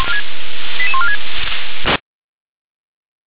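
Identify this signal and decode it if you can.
Five-tone selective call (ZVEI), heard as quick runs of five tones, tones of about 80 ms. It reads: 88146
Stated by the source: ZVEI five-tone call 88085 received on SDR.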